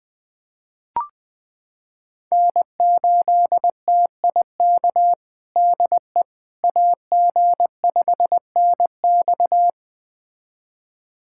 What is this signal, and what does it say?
Morse code, heard as a keyed tone, about 21 wpm, 700 Hz. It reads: N8TIK DE AG5NX
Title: USA Callsign DE Callsign - 5 Characters - 20wpm